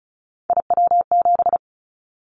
Morse code, heard as a keyed tone, about 36 wpm, 700 Hz. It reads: IW7
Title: Callsign Prefixes for Top 25 DXCC - 35wpm